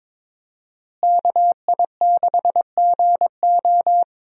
Morse code, heard as a keyed tone, about 22 wpm, 700 Hz. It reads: KI6GO
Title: USA Call Signs - 5 Characters - Repeated 2x - 22wpm